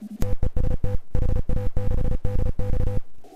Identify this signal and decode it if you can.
Morse code, heard as a keyed tone, about 33 wpm, 500 Hz, with noise. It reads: TEST HA6DX